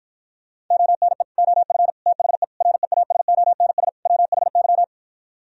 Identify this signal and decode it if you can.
Morse code, heard as a keyed tone, about 52 wpm, 700 Hz. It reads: ONE OF THE REASONS WHY